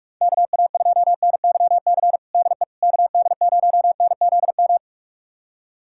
Morse code, heard as a keyed tone, about 46 wpm, 700 Hz. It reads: KA2NYC DE KD0NZM
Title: USA Callsign DE Callsign - 6 Characters - 45wpm